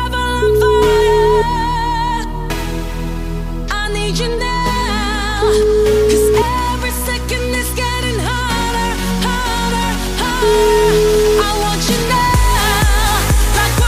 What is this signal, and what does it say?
Ringback tone (Europe, 425 Hz), with noise.